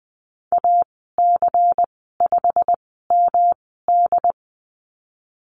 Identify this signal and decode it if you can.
Morse code, heard as a keyed tone, about 20 wpm, spaced as for 15 wpm, 700 Hz. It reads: AC5MD
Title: 179 Lesson 29g - Slash - Call Signs 20wpm